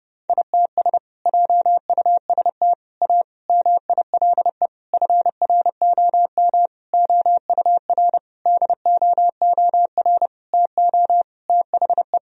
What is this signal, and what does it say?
Morse code, heard as a keyed tone, about 31 wpm, 700 Hz. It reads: ITS JUST A MILE FROM OUR DOOR TO THE
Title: Sentences from Top 1000 Words - Review Part 1 - 30wpm